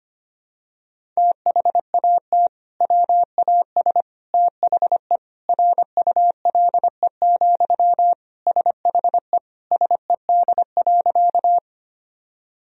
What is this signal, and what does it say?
Morse code, heard as a keyed tone, about 25 wpm, 700 Hz. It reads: THAT WAS THE RULE, SHE SED.
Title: Sentences from Top 400 Words and Abbreviations 25wpm